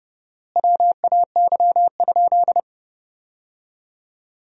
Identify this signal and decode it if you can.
Morse code, heard as a keyed tone, about 30 wpm, 700 Hz. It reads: WAY?
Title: 4x Word-Spacing - Sentences from Top 200 Words - 30wpm